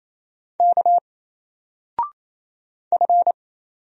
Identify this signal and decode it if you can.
Morse code, heard as a keyed tone, about 27 wpm, 700 Hz. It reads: K F
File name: Single-Letter-Number-28wpm